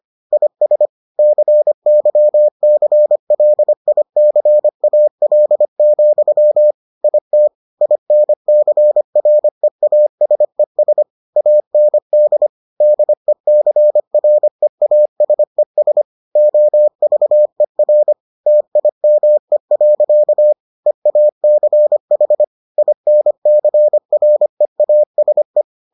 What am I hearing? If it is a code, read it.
Morse code, heard as a keyed tone, about 25 wpm, 600 Hz. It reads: IS CYCLICAL, IT INCREASES AND DECREASES OVER TIME. EACH INCREASE